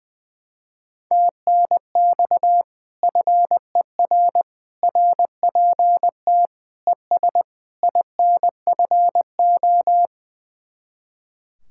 Morse code, heard as a keyed tone, about 20 wpm, 700 Hz. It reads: TNX FER RPT ES INFO
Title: Speed-Racing Common QSO Phrases - 20wpm